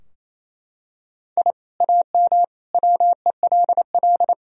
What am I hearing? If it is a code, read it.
Morse code, heard as a keyed tone, about 28 wpm, 700 Hz. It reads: I AM WELL
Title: Sentences from Top 1000 Words - Review Part 3 - 28wpm